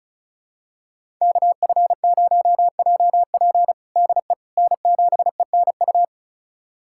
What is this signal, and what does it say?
Morse code, heard as a keyed tone, about 36 wpm, 700 Hz. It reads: KF0JP DE N7ENU
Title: USA Callsign DE Callsign - 5 Characters - 35wpm